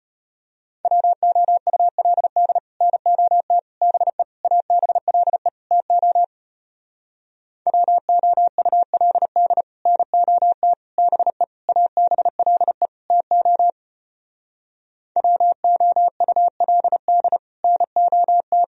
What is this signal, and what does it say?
Morse code, heard as a keyed tone, about 33 wpm, 700 Hz. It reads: WOULD NOT BE ABLE TO WOULD NOT BE ABLE TO WOULD NOT